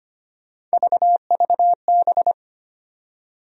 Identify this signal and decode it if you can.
Morse code, heard as a keyed tone, about 25 wpm, 700 Hz. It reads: VVB
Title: USA Call Sign Postfixes - 3 Letters 25wpm